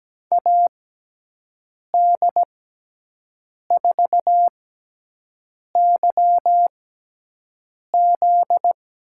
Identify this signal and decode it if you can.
Morse code, heard as a keyed tone, about 17 wpm, spaced as for 6 wpm, 700 Hz. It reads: AD4YZ
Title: USA Call Signs - 6x Character Spacing - 17wpm